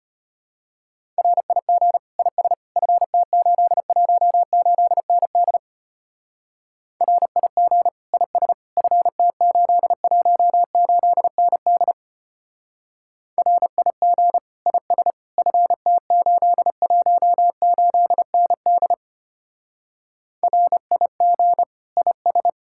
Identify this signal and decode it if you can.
Morse code, heard as a keyed tone, about 32 wpm, 700 Hz. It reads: RIG IS FT818ND RIG IS FT818ND RIG IS FT818ND RIG IS